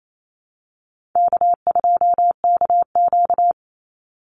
Morse code, heard as a keyed tone, about 28 wpm, 700 Hz. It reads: K2KQ
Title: Easy Call Signs Repeated 2x - 28wpm